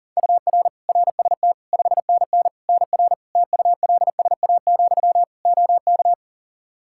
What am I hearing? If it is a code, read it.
Morse code, heard as a keyed tone, about 40 wpm, 700 Hz. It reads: UR RST 5NN NR TULSA, OK
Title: Practice QSO - Exchange 1 - Beginning - 40wpm